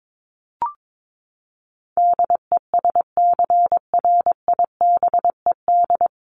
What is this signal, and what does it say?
Morse code, heard as a keyed tone, about 22 wpm, 700 Hz. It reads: DESCRIBED
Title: ED Suffix - Top 100 Words with Base Words 22wpm